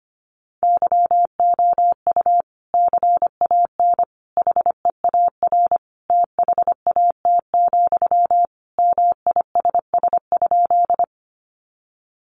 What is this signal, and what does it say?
Morse code, heard as a keyed tone, about 25 wpm, 700 Hz. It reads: YOU CAN HEAR THAT, MISS?
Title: Sentences from Top 500 Words 25wpm